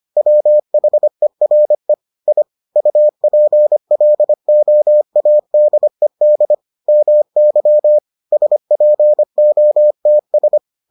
Morse code, heard as a keyed tone, about 25 wpm, 600 Hz. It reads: WHERE I UPLOADED MY SPOTS